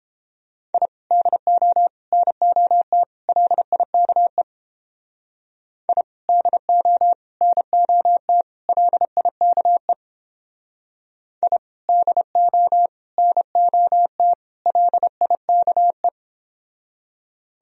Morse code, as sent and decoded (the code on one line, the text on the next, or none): .. / -.. --- / -. --- - / .-.. .. -.- . / .. / -.. --- / -. --- - / .-.. .. -.- . / .. / -.. --- / -. --- - / .-.. .. -.- .
I DO NOT LIKE I DO NOT LIKE I DO NOT LIKE